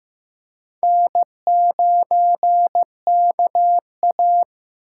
Morse code, heard as a keyed tone, about 15 wpm, 700 Hz. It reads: N9KA
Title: USA Call Signs - 6x Character Spacing - 15wpm